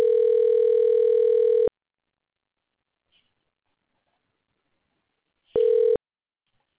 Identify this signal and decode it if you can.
Ringback tone (North America), 440 + 480 Hz.